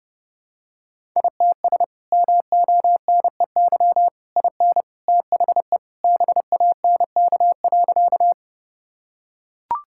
Morse code, heard as a keyed tone, about 32 wpm, 700 Hz. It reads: ITS MONEY IN THE BANK.
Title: Sentences from Top 1000 Words - Review Part 6 - 30wpm